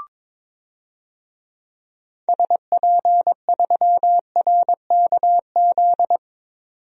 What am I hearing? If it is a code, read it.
Morse code, heard as a keyed tone, about 22 wpm, 700 Hz. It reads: SP3RKZ
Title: DXCC-100 Call Signs - 6 Characters - 22wpm